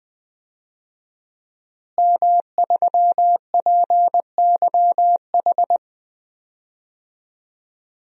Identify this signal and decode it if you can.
Morse code, heard as a keyed tone, about 20 wpm, 700 Hz. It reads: M3PYH
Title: DXCC-100 Call Signs - 5 Characters - Repeated 3x - 20wpm